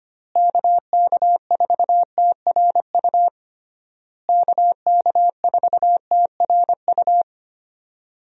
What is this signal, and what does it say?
Morse code, heard as a keyed tone, about 25 wpm, 700 Hz. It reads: KK4TRU KK4TRU
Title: USA Call Signs - 6 Characters - Repeated 3x - 25wpm